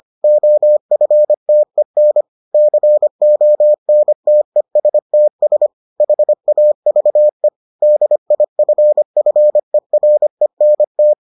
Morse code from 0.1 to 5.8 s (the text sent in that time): OFTEN CONTESTS